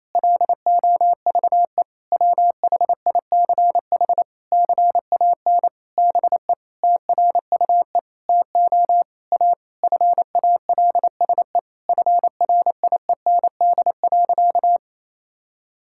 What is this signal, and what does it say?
Morse code, heard as a keyed tone, about 27 wpm, 700 Hz. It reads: LOVE WHICH CAN BE TRUE TO A FALSE FRIEND.
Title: Sentences from Top 1600 Words 28wpm